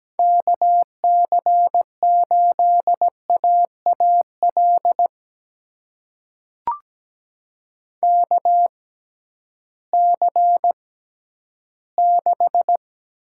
Morse code, heard as a keyed tone, about 17 wpm, spaced as for 11 wpm, 700 Hz. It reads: KC8AAL KC6
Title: USA Call Signs - 6 Characters - 6x Character Spacing - 17wpm